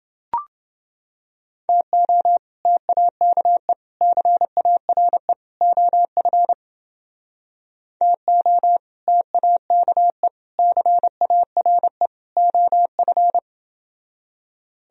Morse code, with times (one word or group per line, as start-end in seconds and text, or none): 1.7-2.4 TO
2.7-3.7 TAKE
4.0-5.3 CARE
5.6-6.5 OF
8.0-8.8 TO
9.1-10.3 TAKE
10.6-12.1 CARE
12.4-13.4 OF